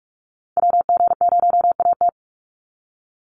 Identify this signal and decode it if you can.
Morse code, heard as a keyed tone, about 44 wpm, 700 Hz. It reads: WG0AT
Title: NA SOTA Chaser Callsigns 45wpm